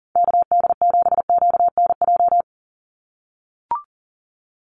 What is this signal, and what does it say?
Morse code, heard as a keyed tone, about 39 wpm, 700 Hz. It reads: KD7QNJ
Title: USA Call Signs 40wpm